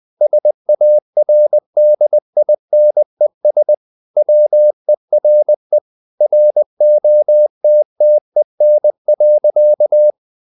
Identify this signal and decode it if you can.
Morse code, heard as a keyed tone, about 20 wpm, 600 Hz. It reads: SARDINES WERE ROTTEN.
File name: the old man and the sea part 2 20wpm